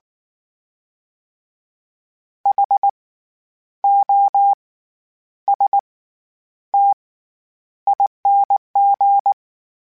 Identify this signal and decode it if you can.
Morse code, heard as a keyed tone, about 19 wpm, spaced as for 9 wpm, 800 Hz. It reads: HOSTING
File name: Special INGs / Special INGs-mp3s / SlowToFaster ing @18